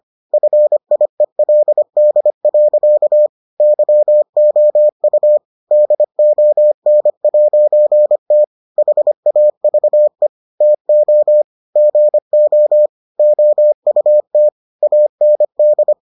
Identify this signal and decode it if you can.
Morse code, heard as a keyed tone, about 25 wpm, 600 Hz. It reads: FIELD. YOU DON'T HAVE TO GO OUT AND